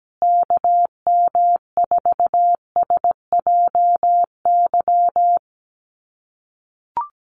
Morse code, heard as a keyed tone, about 17 wpm, 700 Hz. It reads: KM4SJY
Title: USA Call Signs Repeated 2x 17wpm